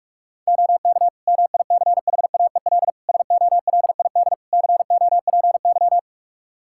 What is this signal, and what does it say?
Morse code, heard as a keyed tone, about 44 wpm, 700 Hz. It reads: OK MICHAEL SOLID COPY
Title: K1USN SST QSO Practice 45wpm